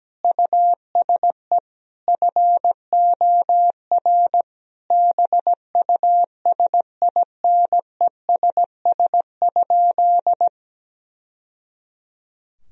Morse code, heard as a keyed tone, about 17 wpm, 700 Hz. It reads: USE FOR BUSINESS?
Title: Sentences from Top 1000 Words - Review Part 3 - 17wpm